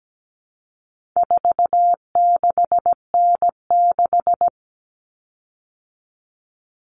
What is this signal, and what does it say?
Morse code, heard as a keyed tone, about 17 wpm, 700 Hz. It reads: 46N6
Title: Common Cut Numbers - 4-Digit 17wpm